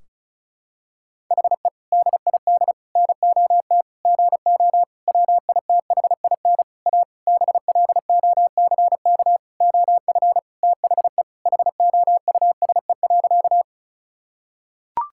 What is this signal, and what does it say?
Morse code, heard as a keyed tone, about 36 wpm, 700 Hz. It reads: HE DID NOT GO WITHIN A BLOCK OF THE HOUSE.
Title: Sentences from Top 1600 Words 35wpm